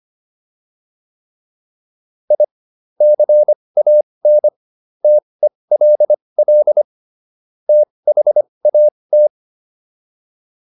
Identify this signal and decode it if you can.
Morse code, heard as a keyed tone, about 25 wpm, spaced as for 20 wpm, 600 Hz. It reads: I CAN TELL THAT